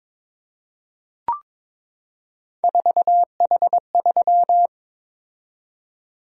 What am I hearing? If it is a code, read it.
Morse code, heard as a keyed tone, about 22 wpm, 700 Hz. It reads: 4H3
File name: practice / 22wpm / Three Letter-Number Combinations - 22wpm